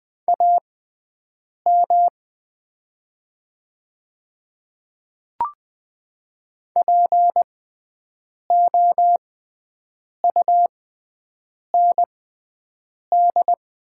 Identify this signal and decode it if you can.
Morse code, heard as a keyed tone, about 20 wpm, spaced as for 7 wpm, 700 Hz. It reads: AM POUND
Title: 6x Character Spacing - Top 500 Words - 20wpm